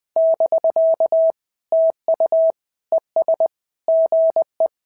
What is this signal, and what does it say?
Morse code, heard as a keyed tone, about 20 wpm, 650 Hz. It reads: <BK> TU ES GE
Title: SOTA QSO Practice 20wpm